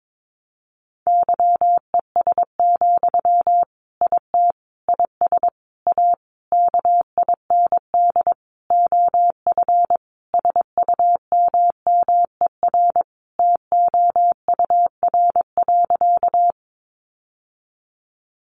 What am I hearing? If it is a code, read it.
Morse code, heard as a keyed tone, about 22 wpm, 700 Hz. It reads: YES, IT IS A KIND OF SUMMER TOUR.
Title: Sentences from Top 1400 Words 22wpm